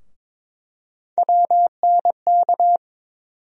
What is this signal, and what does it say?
Morse code, heard as a keyed tone, about 22 wpm, 700 Hz. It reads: WNK
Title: USA Call Sign Postfixes - 3 Letters 22wpm